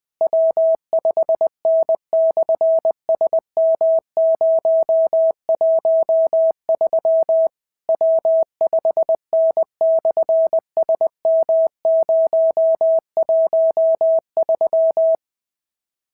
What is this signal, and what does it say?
Morse code, heard as a keyed tone, about 21 wpm, 650 Hz. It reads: W5N/SM013 W5N/SM013
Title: SOTA QSO Practice 20wpm